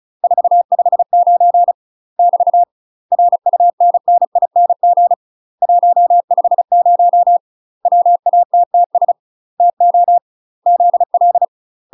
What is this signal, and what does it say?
Morse code, heard as a keyed tone, about 36 wpm, 700 Hz. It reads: 459 <BT> RUNNING 150 WATTS TO ZL
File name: UK MASTER QSOs TRAINING / 35WPM / UK MASTER QSOs 0020